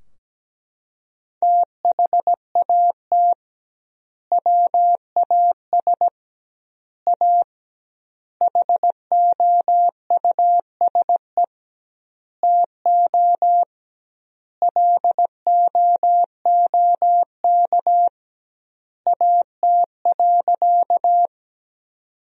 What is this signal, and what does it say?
Morse code, heard as a keyed tone, about 17 wpm, 700 Hz. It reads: THAT WAS A HOUSE TO LOOK AT.
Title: 2x Word-Spacing - Sentences from Top 200 Words - 17wpm